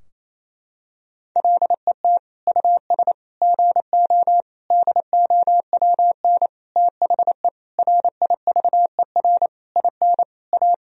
Morse code, heard as a keyed tone, about 27 wpm, 700 Hz. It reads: LET US GO DOWN THE RIVER IN A
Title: Sentences from Top 1000 Words - Review Part 3 - 28wpm